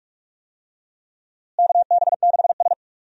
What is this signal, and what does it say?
Morse code, heard as a keyed tone, about 44 wpm, 700 Hz. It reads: KB6S
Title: USA Call Signs - 4 Characters - Repeated 3x - 45wpm